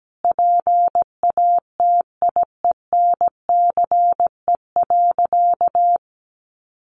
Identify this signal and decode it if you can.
Morse code, heard as a keyed tone, about 17 wpm, 700 Hz. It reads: PATIENCE.
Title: Sentences from Top 1300 Words 17wpm